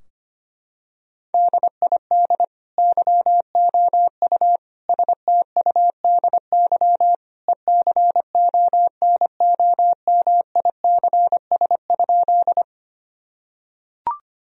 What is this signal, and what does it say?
Morse code, heard as a keyed tone, about 25 wpm, 700 Hz. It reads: DID YOU STUDY ECONOMICS?
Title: Sentences from Top 1000 Words - Review Part 3 - 25wpm